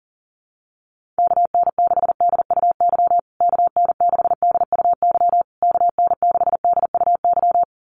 Morse code, heard as a keyed tone, about 40 wpm, 700 Hz. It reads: KN6DUY KN6DUY KN6DUY
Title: USA Call Signs Repeated 3x 40wpm